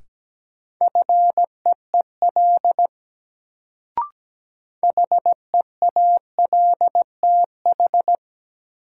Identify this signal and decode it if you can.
Morse code, heard as a keyed tone, about 17 wpm, 700 Hz. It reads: FEEL HEALTH
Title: Sets of 5 Words - New Word Drill - 17wpm